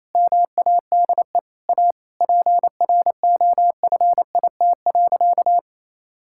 Morse code, heard as a keyed tone, about 28 wpm, 700 Hz. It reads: MADE A PROFIT.
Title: Sentences from Top 1000 Words - Review Part 5 - 28wpm